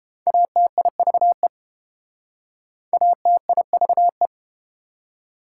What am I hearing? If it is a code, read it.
Morse code, heard as a keyed tone, about 30 wpm, 690 Hz. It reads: ATIVE ATIVE